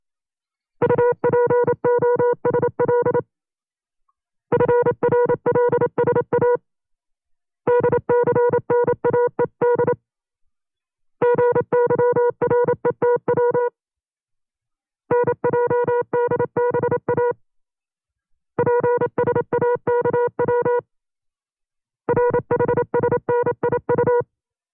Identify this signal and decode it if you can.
Morse code, heard as a keyed tone, about 27 wpm, 480 Hz. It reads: UPOSL FRLSA DCNAED GYRETW NJDBA PSAKW RHSNIU
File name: CW-practice/25WPM